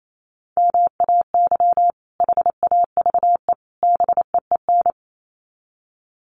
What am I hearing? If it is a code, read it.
Morse code, heard as a keyed tone, about 28 wpm, 700 Hz. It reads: MAY HAVE BEEN
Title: Sets of 3 Words 28wpm